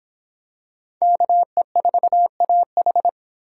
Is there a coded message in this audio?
Yes: Morse code from 1.0 to 3.1 s, heard as a keyed tone, about 22 wpm, 700 Hz. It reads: KE4AH